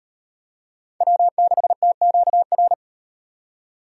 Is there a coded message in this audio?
Yes: Morse code from 1.0 to 2.7 s, heard as a keyed tone, about 38 wpm, 700 Hz. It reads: W6TQR